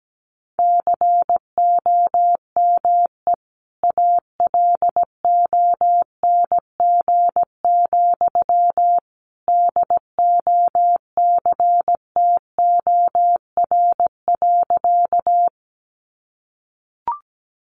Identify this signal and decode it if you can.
Morse code, heard as a keyed tone, about 17 wpm, 700 Hz. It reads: COME ALONG, DOCTOR.
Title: Sentences from Top 1500 Words 17wpm